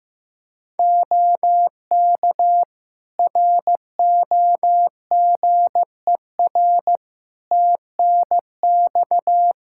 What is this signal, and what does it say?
Morse code, heard as a keyed tone, about 15 wpm, 700 Hz. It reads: OK ROGER TNX
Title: Practice QSO - Exchange 4 - Ending - 15wpm